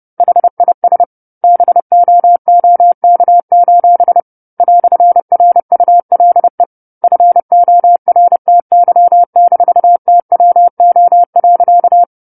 Morse code, heard as a keyed tone, about 30 wpm, 700 Hz. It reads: HIS BOOK: "RULE FORTY-TWO.
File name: 10-Chapter-10 30 WPM